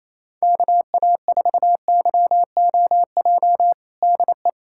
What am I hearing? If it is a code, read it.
Morse code, heard as a keyed tone, about 28 wpm, 700 Hz. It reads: KA4YOJ DE
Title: USA Callsign DE Callsign 28wpm